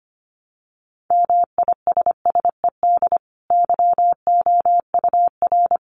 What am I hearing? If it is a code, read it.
Morse code, heard as a keyed tone, about 25 wpm, 700 Hz. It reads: MISSED YOUR
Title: Common QSO Phrases 25wpm